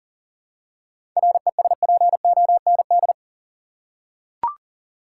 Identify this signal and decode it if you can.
Morse code, heard as a keyed tone, about 40 wpm, 700 Hz. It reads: RESPOND
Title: Seven Letter Words - 40wpm